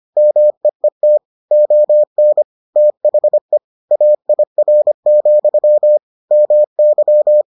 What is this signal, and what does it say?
Morse code, heard as a keyed tone, about 25 wpm, 600 Hz. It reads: MEET ON THE AIR, MY